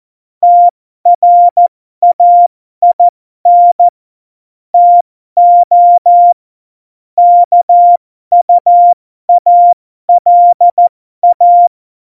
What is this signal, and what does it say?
Morse code, heard as a keyed tone, about 14 wpm, spaced as for 12 wpm, 700 Hz. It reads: TRAIN TO KUALA
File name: Morsekurs / Witze